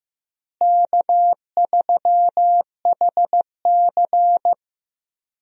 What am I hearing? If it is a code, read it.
Morse code, heard as a keyed tone, about 15 wpm, 700 Hz. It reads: K3HC
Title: USA Call Signs - 4 Characters - Repeated 3x - 15wpm